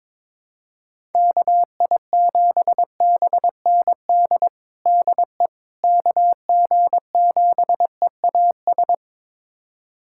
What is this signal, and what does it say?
Morse code, heard as a keyed tone, about 21 wpm, 700 Hz. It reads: KI7BND DE KG7EAS
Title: USA Callsign DE Callsign - 6 Characters - 22wpm